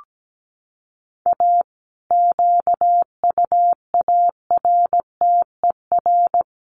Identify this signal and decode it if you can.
Morse code, heard as a keyed tone, about 17 wpm, 700 Hz. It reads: A QUARTER
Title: Sentences from Top 1300 Words 17wpm